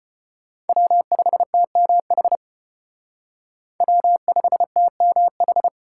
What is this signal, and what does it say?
Morse code, heard as a keyed tone, about 31 wpm, 700 Hz. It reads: W5TMH W5TMH
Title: Speed Racing - USA Call Signs 25wpm